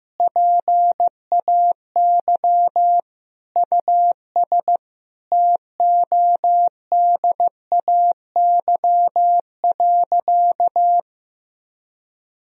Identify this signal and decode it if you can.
Morse code, heard as a keyed tone, about 16 wpm, 700 Hz. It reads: PAY US TODAY.
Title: Sentences from Top 1000 Words - Review Part 4 - 15wpm